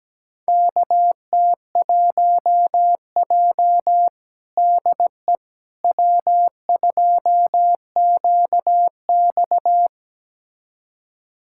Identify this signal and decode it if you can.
Morse code, heard as a keyed tone, about 17 wpm, 700 Hz. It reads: KT1J DE W2QX
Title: USA Callsign DE Callsign - 4 Characters - 17wpm